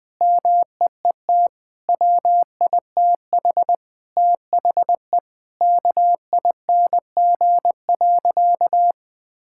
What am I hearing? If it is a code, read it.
Morse code, heard as a keyed tone, about 20 wpm, 700 Hz. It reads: MEET WITH THE KING.
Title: Sentences from Top 1000 Words - Review Part 6 - 20wpm